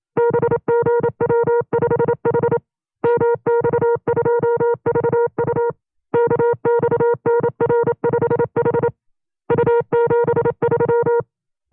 Morse code, heard as a keyed tone, about 27 wpm, 470 Hz. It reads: BGW5H MX2VU KXNR5H U73
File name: FreeFiles / CW-practice / 25WPM-side1